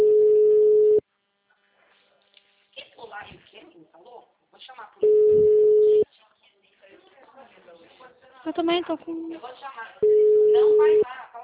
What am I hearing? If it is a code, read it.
Ringback tone (Europe, 425 Hz), with speech.
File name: out- -2021- - - .24194